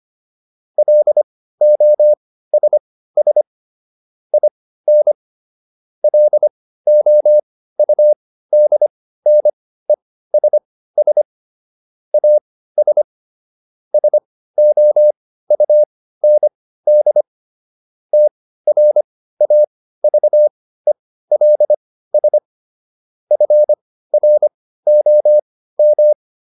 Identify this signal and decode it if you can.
Morse code, heard as a keyed tone, about 25 wpm, spaced as for 15 wpm, 600 Hz. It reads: LOSS IN LOUDNESS AS SOUND TRAVELS FROM